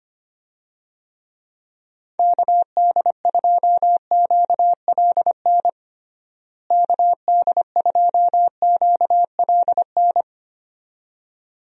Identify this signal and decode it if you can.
Morse code, heard as a keyed tone, about 25 wpm, 700 Hz. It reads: KD2QLN KD2QLN
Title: USA Call Signs - 6 Characters - Repeated 2x - 25wpm